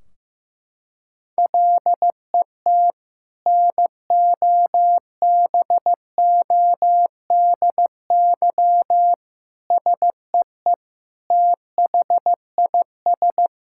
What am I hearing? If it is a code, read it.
Morse code, heard as a keyed tone, about 15 wpm, 700 Hz. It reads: LET NOBODY SEE THIS
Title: Sentences from Top 1600 Words 15wpm